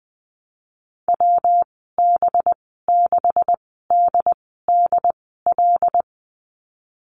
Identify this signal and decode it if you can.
Morse code, heard as a keyed tone, about 19 wpm, spaced as for 14 wpm, 700 Hz. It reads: WB6DDL
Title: 231 Lesson 37g - Letter x - Call Signs 20wpm